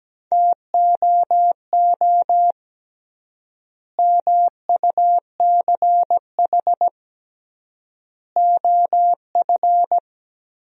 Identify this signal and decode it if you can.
Morse code, heard as a keyed tone, about 17 wpm, 700 Hz. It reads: TOO MUCH OF
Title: 3x Word-Spacing - Sentences from Top 200 Words - 17wpm